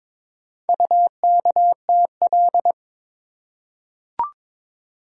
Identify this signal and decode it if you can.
Morse code, heard as a keyed tone, about 23 wpm, 700 Hz. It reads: UKTL
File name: practice / 22wpm / Four Letter-Number Combinations - 22wpm